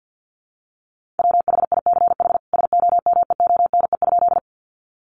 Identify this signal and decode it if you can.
Morse code, heard as a keyed tone, about 50 wpm, 700 Hz. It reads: WHICH SOMEONE?
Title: Sentences from Top 600 Words 50wpm